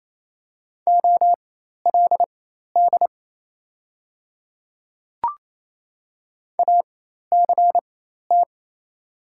Morse code, heard as a keyed tone, about 28 wpm, spaced as for 13 wpm, 700 Hz. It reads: OLD ACT
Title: 4x Character Spacing - Top 300 Words - 28wpm